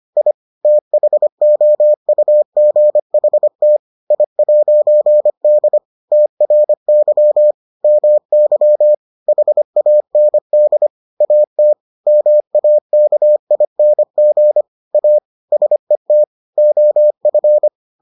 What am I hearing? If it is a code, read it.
Morse code, heard as a keyed tone, about 25 wpm, 600 Hz. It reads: I THOUGHT I'D TRY MY HAND AT MAKING A SET OF